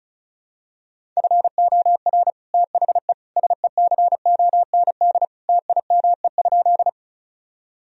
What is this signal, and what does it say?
Morse code, heard as a keyed tone, about 34 wpm, 700 Hz. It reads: FOR THE SECOND TIME?
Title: Sentences from Top 1000 Words - Review Part 1 - 35wpm